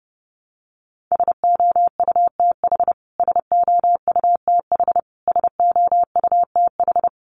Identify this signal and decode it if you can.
Morse code, heard as a keyed tone, about 29 wpm, 700 Hz. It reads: SOUTH SOUTH SOUTH
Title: Top 500 Words Repeated 30wpm